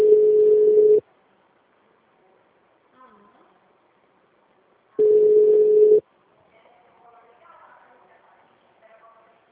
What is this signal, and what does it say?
Ringback tone (Europe, 425 Hz), with speech.
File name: out- -2016- - - .634